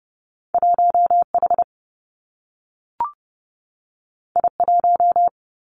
Morse code, heard as a keyed tone, about 30 wpm, 700 Hz. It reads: 1 H I 1
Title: Two Challenging Letter-Number Combinations 30wpm